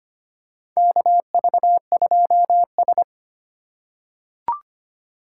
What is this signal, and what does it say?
Morse code, heard as a keyed tone, about 25 wpm, 700 Hz. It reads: KV2S
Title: USA Call Signs - 4x Character Spacing - 25wpm